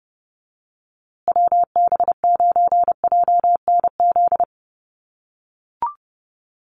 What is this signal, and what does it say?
Morse code, heard as a keyed tone, about 30 wpm, 700 Hz. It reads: WB9JNZ
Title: USA Call Signs Repeated 2x 30wpm